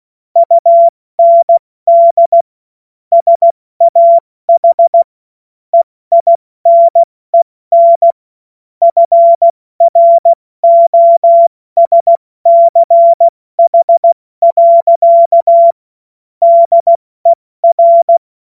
Morse code, heard as a keyed tone, about 16 wpm, 680 Hz. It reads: UND SAH EINEN FROSCH. DER
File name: Morsekurs / Witze / Witz 08. November 2022 70 F80